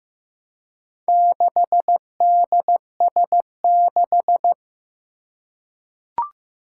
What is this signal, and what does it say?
Morse code, heard as a keyed tone, about 15 wpm, 700 Hz. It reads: 6DS6
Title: Four Challenging Letter-Number Combinations 15wpm